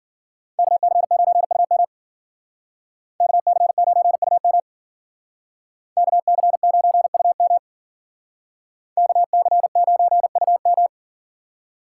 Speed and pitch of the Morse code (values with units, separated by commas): 52 wpm, 700 Hz